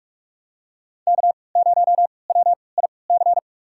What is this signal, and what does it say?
Morse code, heard as a keyed tone, about 42 wpm, spaced as for 25 wpm, 700 Hz. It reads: K0WIC
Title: USA Call Signs - 5 Characters - 3x Character Spacing - 45wpm